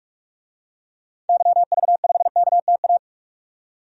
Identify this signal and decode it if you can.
Morse code, heard as a keyed tone, about 46 wpm, 700 Hz. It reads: YV5KTA
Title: DXCC-100 Call Signs - 6 Characters - 45wpm